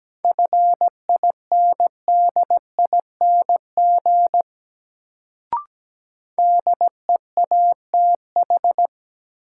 Morse code, heard as a keyed tone, about 17 wpm, 700 Hz. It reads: FINDING DEATH
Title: Sentences from Top 1100 Words - New Word Drill Alt - 17wpm